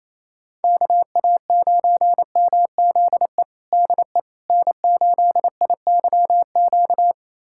Morse code, heard as a keyed tone, about 28 wpm, 700 Hz. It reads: KA9MZE DE N8IYQ
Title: USA Callsign DE Callsign 28wpm